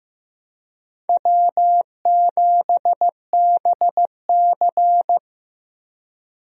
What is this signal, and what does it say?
Morse code, heard as a keyed tone, about 15 wpm, 700 Hz. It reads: W7BC
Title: USA Call Signs - 4 Characters - 15wpm